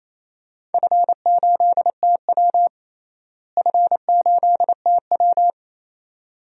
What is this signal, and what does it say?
Morse code, heard as a keyed tone, about 28 wpm, 700 Hz. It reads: F8TW F8TW
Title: DXCC-100 Call Signs - 4 Characters - Repeated 2x - 28wpm